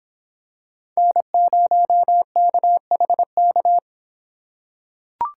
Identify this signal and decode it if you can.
Morse code, heard as a keyed tone, about 26 wpm, 700 Hz. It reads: N0KHK